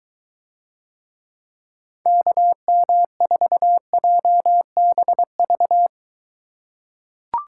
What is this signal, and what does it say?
Morse code, heard as a keyed tone, about 21 wpm, 700 Hz. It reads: KM4JBV